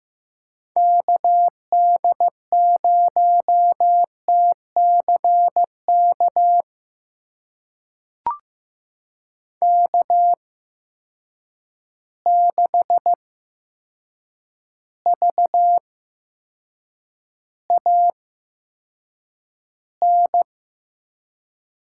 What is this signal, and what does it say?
Morse code, heard as a keyed tone, about 15 wpm, spaced as for 7 wpm, 700 Hz. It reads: KD0TCK K6VAN
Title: USA Call Signs - 8x Character Spacing - 15wpm